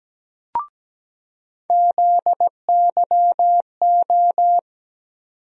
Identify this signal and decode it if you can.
Morse code, heard as a keyed tone, about 17 wpm, 700 Hz. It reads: ZYO
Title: USA Call Sign Postfixes - 3 Letters 17wpm